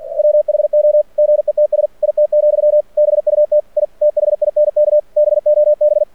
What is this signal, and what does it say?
Morse code, heard as a keyed tone, about 48 wpm, 600 Hz. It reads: WHO GETS IT, BUT I THINK BOB
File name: archived / CRSnet